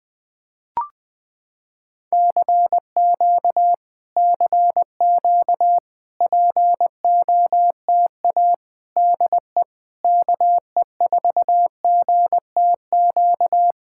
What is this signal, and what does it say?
Morse code, heard as a keyed tone, about 20 wpm, 700 Hz. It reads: CQ CQ POTA DE KE4GTQ
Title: POTA QSO Practice - 20wpm